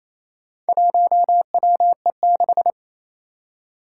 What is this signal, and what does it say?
Morse code, heard as a keyed tone, about 28 wpm, 700 Hz. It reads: 1WE6
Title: Four Challenging Letter-Number Combinations 28wpm